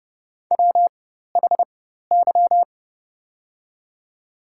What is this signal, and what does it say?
Morse code, heard as a keyed tone, about 30 wpm, spaced as for 14 wpm, 700 Hz. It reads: WHY
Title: 4x Character Spacing - Top 200 Words - 30wpm